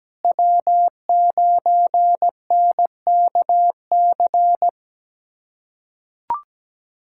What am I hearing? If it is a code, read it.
Morse code, heard as a keyed tone, about 17 wpm, 700 Hz. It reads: W9NKC
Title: USA Callsign DE Callsign - 5 Characters - 17wpm